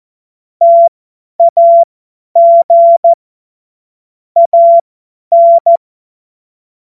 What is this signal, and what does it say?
Morse code, heard as a keyed tone, about 14 wpm, spaced as for 10 wpm, 680 Hz. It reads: TAG AN
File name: Morsekurs / Witze / Witz 21. März 2023 50 F70